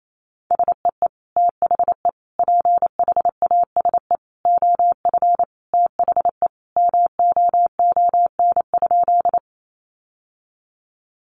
Morse code, heard as a keyed tone, about 28 wpm, 700 Hz. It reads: SEE THE PHASE OF THE MOON?
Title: Sentences from Top 1200 Words 28wpm